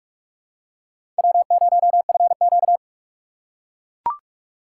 Morse code, heard as a keyed tone, about 42 wpm, 700 Hz. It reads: W0FQ
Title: USA Call Signs - 4 Characters - 4x Character Spacing - 45wpm